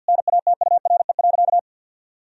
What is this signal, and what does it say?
Morse code, heard as a keyed tone, about 50 wpm, 700 Hz. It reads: NATURE.
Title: Sentences from Top 1000 Words - Review Part 4 - 50wpm